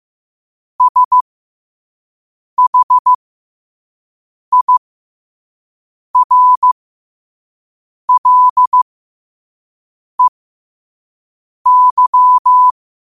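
Morse code, heard as a keyed tone, about 15 wpm, spaced as for 5 wpm, 1000 Hz. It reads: SHIRLEY
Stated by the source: Morse code sent at 5 words per minute.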